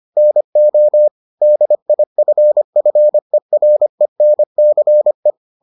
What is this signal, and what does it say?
Morse code, heard as a keyed tone, about 25 wpm, 600 Hz. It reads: NO DIFFERENCE